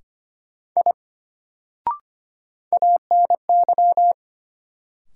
Morse code, heard as a keyed tone, about 25 wpm, 700 Hz. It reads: I ANY
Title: Sentences from Top 200 Words - New Word Drill - 25wpm